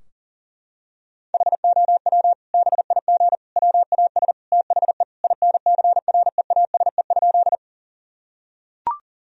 Morse code, heard as a keyed tone, about 37 wpm, 700 Hz. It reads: HOW BIG WAS THE INCREASE?
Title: Sentences from Top 1000 Words - Review Part 4 - 40wpm